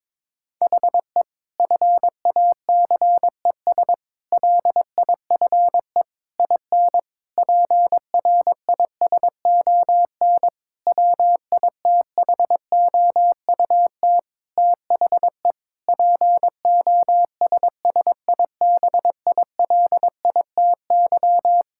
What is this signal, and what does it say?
Morse code, heard as a keyed tone, about 22 wpm, 700 Hz. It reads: HE FACES LIFE IN PRISON WITHOUT THE POSSIBILITY
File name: practice / 22wpm / Sentences with 8-Word Phrases - 22wpm